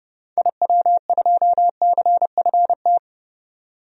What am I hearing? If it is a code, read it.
Morse code, heard as a keyed tone, about 30 wpm, 700 Hz. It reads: IW2CFT
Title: DXCC-100 Call Signs - 6 Characters - 30wpm